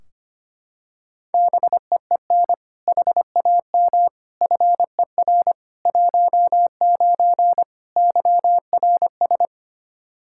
Morse code, heard as a keyed tone, about 25 wpm, 700 Hz. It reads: BEEN HAM FER 19 YRS
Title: Practice QSO - Exchange 3 - More Info - 25wpm